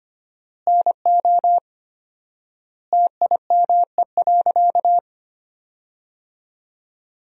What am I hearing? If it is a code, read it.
Morse code, heard as a keyed tone, about 25 wpm, 700 Hz. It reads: NO TIME.
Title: 4x Word-Spacing - Sentences from Top 100 Words - 25wpm